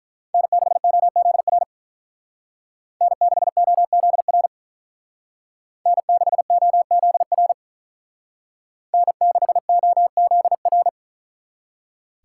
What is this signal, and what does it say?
Morse code, heard as a keyed tone, about 43 wpm, 700 Hz. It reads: N6OZR N6OZR N6OZR N6OZR
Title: Speed Racing - USA 1x3 Call Signs 35wpm